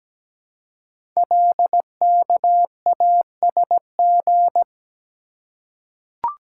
Morse code, heard as a keyed tone, about 17 wpm, 700 Hz. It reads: LKASG
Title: Code Groups - 5 Letters - 17wpm